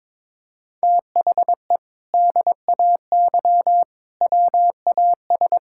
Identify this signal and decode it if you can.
Morse code, heard as a keyed tone, about 21 wpm, 700 Hz. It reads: THE DAY WAS
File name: practice / 22wpm / Sentences with 2-Word Phrases - 22wpm